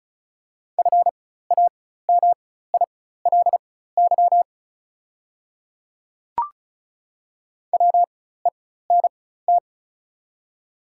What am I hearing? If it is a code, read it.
Morse code, heard as a keyed tone, about 36 wpm, spaced as for 16 wpm, 690 Hz. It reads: FAMILY WENT
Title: 4x Character Spacing - Top 500 Words - 35wpm